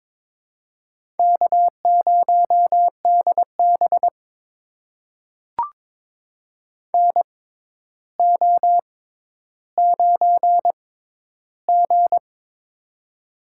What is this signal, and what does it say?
Morse code, heard as a keyed tone, about 22 wpm, spaced as for 11 wpm, 700 Hz. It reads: K0DB NO9G
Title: USA Call Signs - 4 Characters - 6x Character Spacing - 22wpm